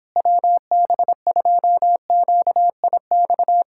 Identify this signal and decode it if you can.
Morse code, heard as a keyed tone, about 26 wpm, 700 Hz. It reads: WB2QIX
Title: Speed-Racing Common QSO Phrases - 22wpm